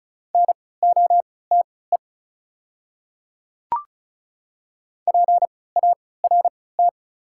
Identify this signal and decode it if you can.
Morse code, heard as a keyed tone, about 34 wpm, spaced as for 20 wpm, 690 Hz. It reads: NOTE PART
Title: 3x Character Spacing - Top 500 Words - 35wpm